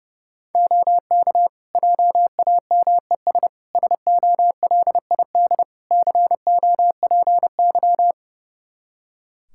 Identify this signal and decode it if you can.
Morse code, heard as a keyed tone, about 32 wpm, 700 Hz. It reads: OK JAMES SOLID COPY
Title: Practice QSO - Exchange 3 - More Info - 30wpm